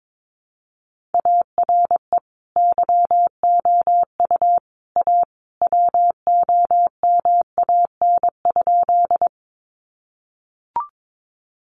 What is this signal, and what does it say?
Morse code, heard as a keyed tone, about 22 wpm, 700 Hz. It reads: ARE YOU A WOMAN?
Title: Sentences from Top 1100 Words 22wpm